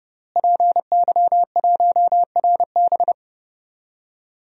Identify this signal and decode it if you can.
Morse code, heard as a keyed tone, about 30 wpm, 700 Hz. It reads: PY1RB
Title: DXCC-100 Call Signs - 5 Characters - Repeated 2x - 30wpm